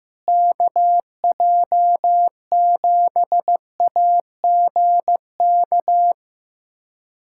Morse code, heard as a keyed tone, about 15 wpm, 700 Hz. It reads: KJ7AGK
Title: USA Callsign DE Callsign 15wpm